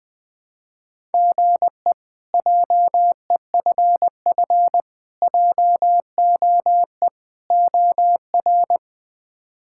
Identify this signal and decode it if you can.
Morse code, heard as a keyed tone, about 20 wpm, 700 Hz. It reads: GE JEFF JOE OR
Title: K1USN SST QSO Practice - 20wpm